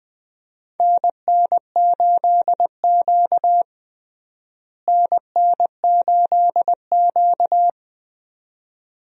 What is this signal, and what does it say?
Morse code, heard as a keyed tone, about 20 wpm, 700 Hz. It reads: NN8Q NN8Q
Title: USA Call Signs - 4 Characters - Repeated 2x - 20wpm